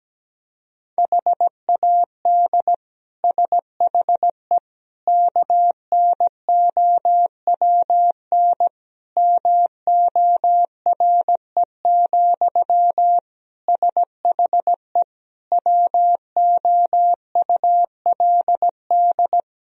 Morse code, heard as a keyed tone, about 17 wpm, 700 Hz. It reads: HAD SHE KNOWN MORE, SHE WOULD
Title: Sentences from Top 1000 Words - Review Part 3 - 17wpm